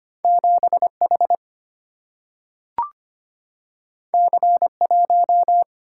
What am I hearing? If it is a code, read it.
Morse code, heard as a keyed tone, about 25 wpm, 700 Hz. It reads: 7 H C 1
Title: Single-Letter Single-Number 25wpm